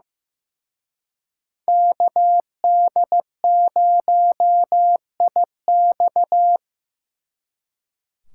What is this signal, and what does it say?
Morse code, heard as a keyed tone, about 18 wpm, 700 Hz. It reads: KD0IX
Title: Speed Racing - USA 2x2 Call Signs 15wpm